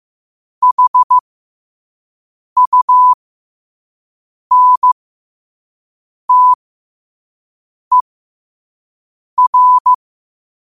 Morse code, heard as a keyed tone, about 15 wpm, spaced as for 5 wpm, 1000 Hz. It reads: HUNTER
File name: qotdfiles / 5wpm